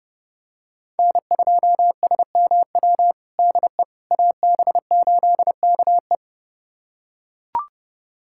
Morse code, heard as a keyed tone, about 32 wpm, 700 Hz. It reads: N2SMW DE AB8KE
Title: USA Callsign DE Callsign - 5 Characters - 30wpm